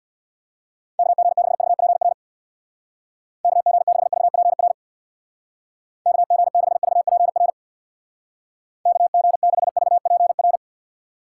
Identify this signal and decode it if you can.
Morse code, heard as a keyed tone, about 61 wpm, 700 Hz. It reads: KG6VPR KG6VPR KG6VPR KG6VPR
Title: Speed Racing - USA 2x3 Call Signs 50wpm